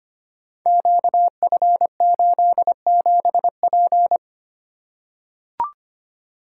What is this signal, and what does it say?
Morse code, heard as a keyed tone, about 25 wpm, 700 Hz. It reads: QF87P
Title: Code Groups - 5 Letter:Numbers - 25wpm